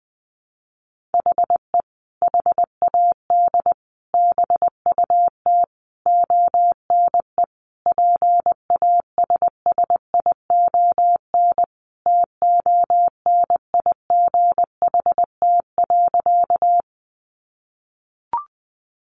Morse code, heard as a keyed tone, about 20 wpm, 700 Hz. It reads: HE HAD BUT ONE PASSION TONIGHT.
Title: Sentences from Top 1300 Words 20wpm